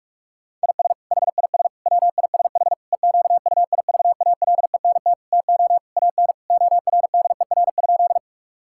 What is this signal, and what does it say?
Morse code, heard as a keyed tone, about 46 wpm, 700 Hz. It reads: IS HIS WISH EQUIVALENT TO AN ORDER?
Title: Sentences from Top 1600 Words 45wpm